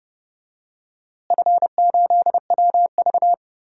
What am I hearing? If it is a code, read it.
Morse code, heard as a keyed tone, about 30 wpm, 700 Hz. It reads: F8WV
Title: DXCC-100 Call Signs - 4 Characters - Repeated 3x - 30wpm